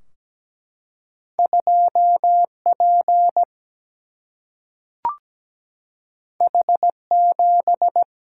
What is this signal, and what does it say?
Morse code, heard as a keyed tone, about 17 wpm, 700 Hz. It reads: 2P H7
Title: Single-Letter Single-Number 17wpm